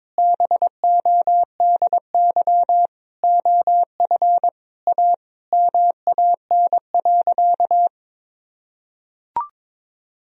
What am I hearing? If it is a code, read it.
Morse code, heard as a keyed tone, about 22 wpm, 700 Hz. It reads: BODY OF A MAN.
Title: Sentences from Top 1000 Words - Review Part 2 - 22wpm